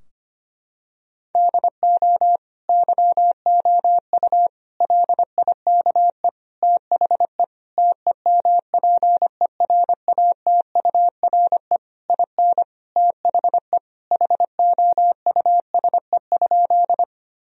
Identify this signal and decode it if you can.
Morse code, heard as a keyed tone, about 25 wpm, 700 Hz. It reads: DO YOU LIKE THE TEMPERATURE IN THE HOUSE?
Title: Sentences from Top 1000 Words - Review Part 4 - 25wpm